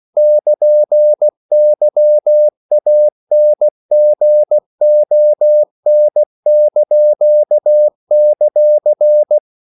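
Morse code, heard as a keyed tone, about 16 wpm, 600 Hz. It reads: <KN>YANGON);
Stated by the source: CW practice of national capitals at 16 words per minute, from ON6ZQ.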